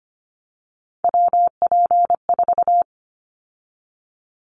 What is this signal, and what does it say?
Morse code, heard as a keyed tone, about 25 wpm, 700 Hz. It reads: WP4
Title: Callsign Prefixes for Top 25 DXCC - 25wpm